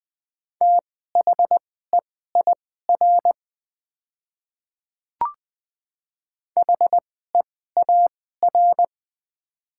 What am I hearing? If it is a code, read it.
Morse code, heard as a keyed tone, about 20 wpm, spaced as for 15 wpm, 700 Hz. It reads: THEIR HEAR
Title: 2x Character Spacing - Top 300 Words - 20wpm